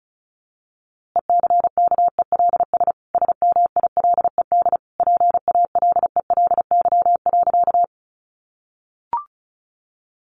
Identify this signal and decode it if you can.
Morse code, heard as a keyed tone, about 34 wpm, 700 Hz. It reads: ECKELS SMILED PALELY.
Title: A Sound Of Thunder 35wpm